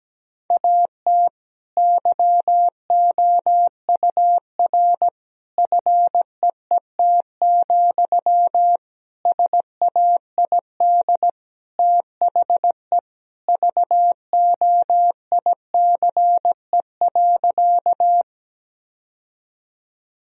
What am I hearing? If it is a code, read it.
Morse code, heard as a keyed tone, about 17 wpm, 700 Hz. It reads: AT YOUR FEET, SAID THE VOICE.
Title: Sentences from Top 1000 Words - Review Part 1 - 17wpm